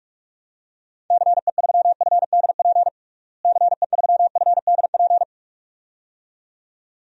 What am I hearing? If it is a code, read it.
Morse code, heard as a keyed tone, about 42 wpm, 700 Hz. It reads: CE3FDP CE3FDP
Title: DXCC-100 Call Signs - 6 Characters - Repeated 2x - 45wpm